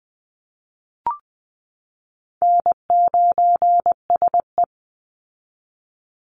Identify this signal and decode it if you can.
Morse code, heard as a keyed tone, about 20 wpm, 700 Hz. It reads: N9SE
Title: Easy Call Signs Repeated 2x - 20wpm